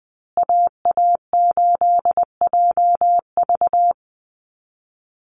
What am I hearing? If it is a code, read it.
Morse code, heard as a keyed tone, about 20 wpm, 700 Hz. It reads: AA8JV
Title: USA Call Signs Repeated 2x 20wpm